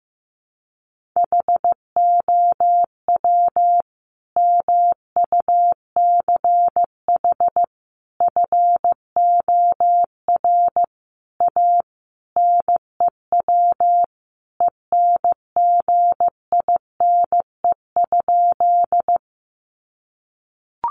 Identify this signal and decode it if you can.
Morse code, heard as a keyed tone, about 15 wpm, 700 Hz. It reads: HOW MUCH FOR A NEW ENGINE?
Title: Sentences from Top 1500 Words 15wpm